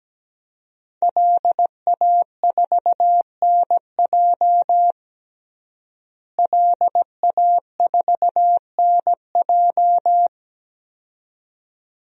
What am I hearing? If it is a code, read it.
Morse code, heard as a keyed tone, about 17 wpm, 700 Hz. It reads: LA4NJ LA4NJ
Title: DXCC-100 Call Signs - 5 Characters - Repeated 2x - 17wpm